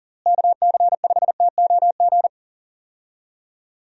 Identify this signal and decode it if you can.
Morse code, heard as a keyed tone, about 40 wpm, 700 Hz. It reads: KC5TOG
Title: USA Call Signs - 6 Characters - Repeated 2x - 40wpm